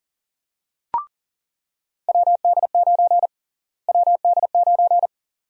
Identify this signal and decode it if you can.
Morse code, heard as a keyed tone, about 40 wpm, 700 Hz. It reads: WD9 WD9
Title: USA Call Sign Prefixes - Repeated 2x - 40wpm